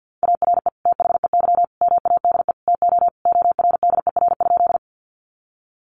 Morse code, heard as a keyed tone, about 49 wpm, 700 Hz. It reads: ARE THEY MADE TO ORDER?
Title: Sentences from Top 400 Words 50wpm